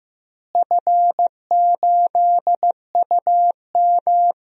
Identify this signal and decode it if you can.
Morse code, heard as a keyed tone, about 15 wpm, 700 Hz. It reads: F8UM
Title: DXCC-100 Call Signs - 4 Characters - Repeated 2x - 15wpm